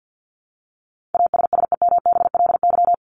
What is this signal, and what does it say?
Morse code, heard as a keyed tone, about 50 wpm, 700 Hz. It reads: ASSEMBLY
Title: LY Suffix - Top 100 Words with Base Words 50wpm